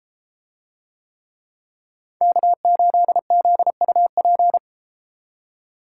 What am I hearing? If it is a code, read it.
Morse code, heard as a keyed tone, about 33 wpm, 700 Hz. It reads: K8ZUP